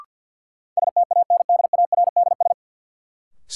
Morse code, heard as a keyed tone, about 53 wpm, 700 Hz. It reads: STANDARDS
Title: Sentences from Top 1000 Words - Review Part 6 - New Word Drill - 50wpm